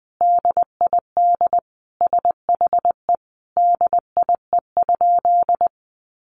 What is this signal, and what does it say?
Morse code, heard as a keyed tone, about 20 wpm, 700 Hz. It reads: DID SHE DIE?
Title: Sentences from Top 1300 Words 20wpm